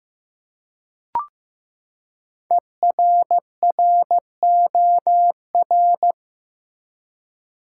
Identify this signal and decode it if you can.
Morse code, heard as a keyed tone, about 15 wpm, 700 Hz. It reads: ERROR
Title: Five Letter Words - 15wpm